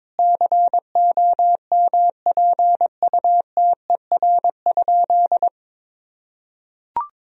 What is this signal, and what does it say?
Morse code, heard as a keyed tone, about 22 wpm, 700 Hz. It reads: COMPUTER?
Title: Sentences from Top 1000 Words - Review Part 3 - 22wpm